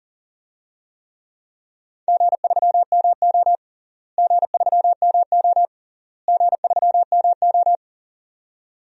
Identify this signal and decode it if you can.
Morse code, heard as a keyed tone, about 40 wpm, 700 Hz. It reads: G3MO G3MO G3MO
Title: DXCC-100 Call Signs - 4 Characters - Repeated 3x - 40wpm